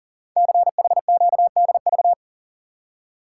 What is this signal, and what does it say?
Morse code, heard as a keyed tone, about 43 wpm, 700 Hz. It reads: CHQDV
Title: Code Groups - 5 Letter:Numbers - 40wpm